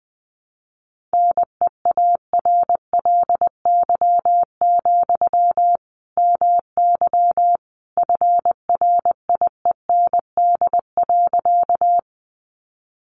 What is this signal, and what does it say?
Morse code, heard as a keyed tone, about 20 wpm, 700 Hz. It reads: NEARLY, MY FRIEND.
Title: Sentences from Top 1300 Words 20wpm